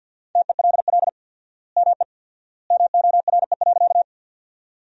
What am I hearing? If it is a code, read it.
Morse code, heard as a keyed tone, about 53 wpm, 700 Hz. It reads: TELL ME MORE.
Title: 4x Word-Spacing - Sentences from Top 300 Words - 50wpm